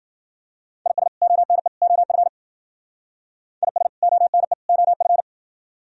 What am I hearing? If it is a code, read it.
Morse code, heard as a keyed tone, about 51 wpm, 700 Hz. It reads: IS ONE OF IS ONE OF